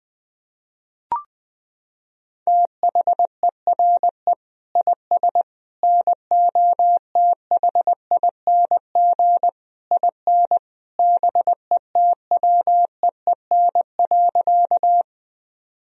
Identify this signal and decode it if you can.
Morse code, heard as a keyed tone, about 21 wpm, 700 Hz. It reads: THERE IS NOTHING IN BETWEEN.
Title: Sentences from Top 1000 Words - Review Part 1 - 20wpm